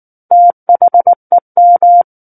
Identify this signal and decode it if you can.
Morse code, heard as a keyed tone, about 19 wpm, 700 Hz. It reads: THEM
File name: Book One / 10-Chapter 10 - 19 WPM